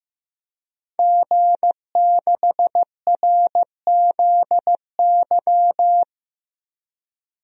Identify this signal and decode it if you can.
Morse code, heard as a keyed tone, about 15 wpm, 700 Hz. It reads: G6RZY
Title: DXCC-100 Call Signs - 5 Characters - 15wpm